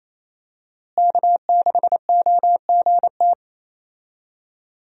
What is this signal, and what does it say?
Morse code, heard as a keyed tone, about 27 wpm, 700 Hz. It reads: K6OGT
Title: USA Call Signs - 5 Characters - Repeated 3x - 28wpm